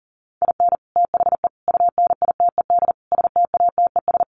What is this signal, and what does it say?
Morse code, heard as a keyed tone, about 40 wpm, 700 Hz. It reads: IN THE UNITED STATES
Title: Sets of 4 Words 40wpm